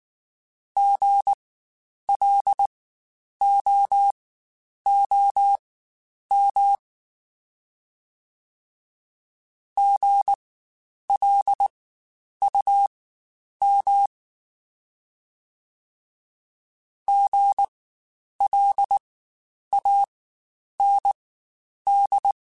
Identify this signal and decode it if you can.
Morse code, heard as a keyed tone, about 19 wpm, spaced as for 9 wpm, 800 Hz. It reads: GLOOM GLUM GLAND
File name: Spaced Alphas / Spaced Alphas mp3s / Gs sp @18